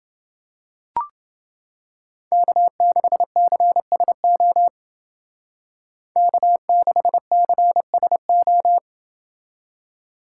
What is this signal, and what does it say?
Morse code, heard as a keyed tone, about 28 wpm, 700 Hz. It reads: K6CSO K6CSO